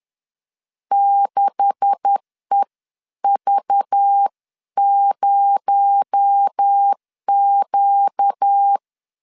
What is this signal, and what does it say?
Morse code, heard as a keyed tone, about 10 wpm, 800 Hz. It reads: 6EV0Q
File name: LEZIONI / 10 SIGNS / 10C DOT 10wpm 2940